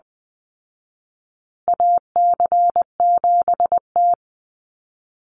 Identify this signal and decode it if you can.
Morse code, heard as a keyed tone, about 20 wpm, 700 Hz. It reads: AC7T
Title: Call Signs Repeated 2x - 20wpm